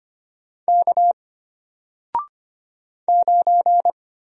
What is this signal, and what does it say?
Morse code, heard as a keyed tone, about 25 wpm, 700 Hz. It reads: K 9
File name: Single-Letter-Number-25wpm